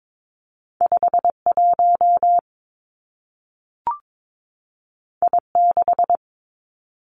Morse code, heard as a keyed tone, about 22 wpm, 700 Hz. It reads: 5 1 I 6
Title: Two Challenging Letter-Number Combinations 22wpm